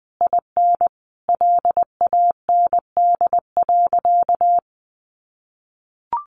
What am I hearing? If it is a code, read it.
Morse code, heard as a keyed tone, about 20 wpm, 700 Hz. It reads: IN LAND.
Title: Sentences from Top 1100 Words 20wpm